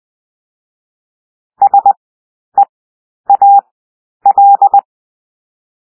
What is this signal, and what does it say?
Morse code, heard as a keyed tone, about 20 wpm, 800 Hz. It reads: S E A L
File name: cwmp3 / W1 SOUND WORDS 20 wpm